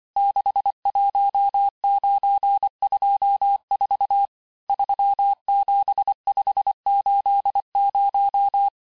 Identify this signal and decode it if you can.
Morse code, heard as a keyed tone, about 25 wpm, 800 Hz. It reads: 61924 37580
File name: texte-chiffre-vitesse-23mpm